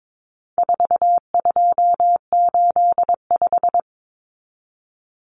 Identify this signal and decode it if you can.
Morse code, heard as a keyed tone, about 22 wpm, 700 Hz. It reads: 4285
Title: Common Cut Numbers - 4-Digit 22wpm